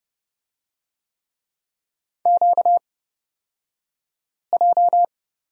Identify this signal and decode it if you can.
Morse code, heard as a keyed tone, about 31 wpm, spaced as for 5 wpm, 700 Hz. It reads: QJ
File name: Lesson 38d - Letter q - Two Characters 30wpm